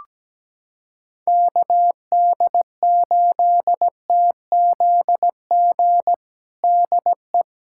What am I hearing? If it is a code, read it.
Morse code, heard as a keyed tone, about 17 wpm, 700 Hz. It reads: KD8TZG DE
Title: USA Callsign DE Callsign - 6 Characters - 17wpm